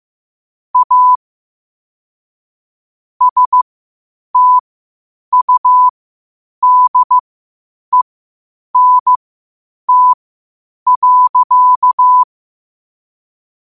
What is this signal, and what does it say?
Morse code, heard as a keyed tone, about 15 wpm, spaced as for 8 wpm, 1000 Hz. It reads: A STUDENT.
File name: qotdfiles / 8wpm